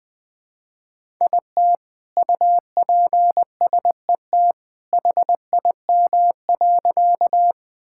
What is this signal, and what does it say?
Morse code, heard as a keyed tone, about 20 wpm, 700 Hz. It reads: IT UPSET HIM.
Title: Sentences from Top 1000 Words - Review Part 5 - 20wpm